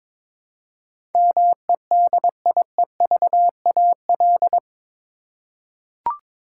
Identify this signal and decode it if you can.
Morse code, heard as a keyed tone, about 22 wpm, 700 Hz. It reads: MEDIEVAL
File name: practice / 22wpm / Eight Letter Words - 22wpm